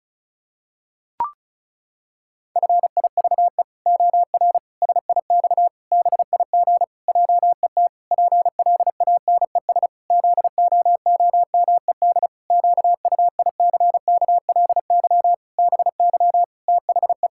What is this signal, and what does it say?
Morse code, heard as a keyed tone, about 36 wpm, 700 Hz. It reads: FIVE OR SIX BIG JET PLANES ZOOMED QUICKLY BY THE
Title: Pangram Challenge v2 - 35wpm